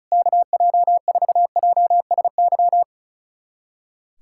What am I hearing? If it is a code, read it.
Morse code, heard as a keyed tone, about 34 wpm, 700 Hz. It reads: KJ4JSY
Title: USA Callsign DE Callsign - 6 Characters - 35wpm